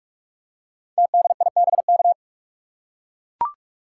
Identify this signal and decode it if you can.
Morse code, heard as a keyed tone, about 44 wpm, 700 Hz. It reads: TDIBK
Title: Code Groups - 5 Letters - 45wpm